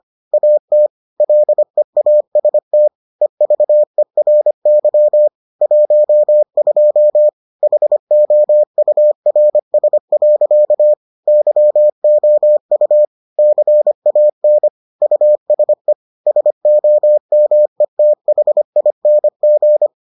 Morse code, heard as a keyed tone, about 25 wpm, 600 Hz. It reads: AT LEAST EVERY 12 HOURS. YOU CAN USE SOMETHING